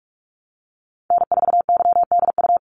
Morse code, heard as a keyed tone, about 46 wpm, 700 Hz. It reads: N4YDU
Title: Call Signs - 45wpm